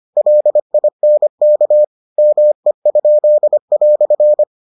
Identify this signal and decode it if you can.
Morse code, heard as a keyed tone, about 25 wpm, 600 Hz. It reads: LINK ME?"